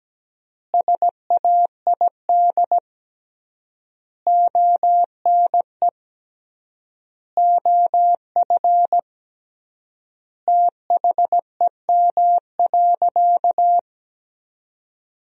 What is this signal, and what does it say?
Morse code, heard as a keyed tone, about 17 wpm, 700 Hz. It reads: SAID ONE OF THEM.
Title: 3x Word-Spacing - Sentences from Top 300 Words - 17wpm